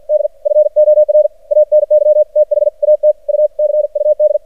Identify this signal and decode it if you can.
Morse code, heard as a keyed tone, about 53 wpm, 610 Hz. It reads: D FOR ANY THAT U CUD
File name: archived / CRSnet